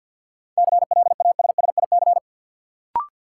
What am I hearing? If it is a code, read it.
Morse code, heard as a keyed tone, about 53 wpm, 700 Hz. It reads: CLASSIC
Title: Sentences from Top 1000 Words - New Word Drill - 50wpm